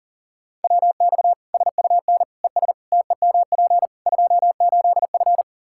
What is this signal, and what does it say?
Morse code, heard as a keyed tone, about 40 wpm, 700 Hz. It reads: WX SUN ES TEMP 28F
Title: Practice QSO - Exchange 2 - Info - 40wpm